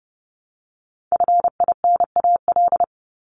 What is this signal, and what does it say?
Morse code, heard as a keyed tone, about 30 wpm, 700 Hz. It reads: FINAL
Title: Top 500 Words Repeated 30wpm